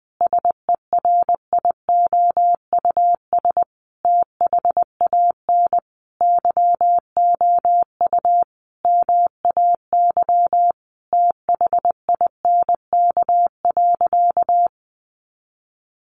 Morse code, heard as a keyed tone, about 20 wpm, 700 Hz. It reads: SERIOUS THAN YOU MAY THINK.
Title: Sentences from Top 1100 Words 20wpm